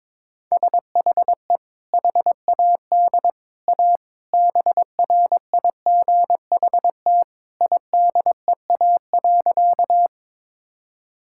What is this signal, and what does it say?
Morse code, heard as a keyed tone, about 21 wpm, 700 Hz. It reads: SHE HAD A BRIGHT IDEA.
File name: practice / 22wpm / Sentences with 2-Word Phrases - 22wpm